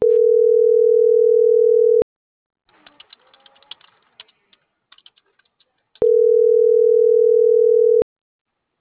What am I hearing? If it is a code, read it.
Ringback tone (North America), 440 + 480 Hz.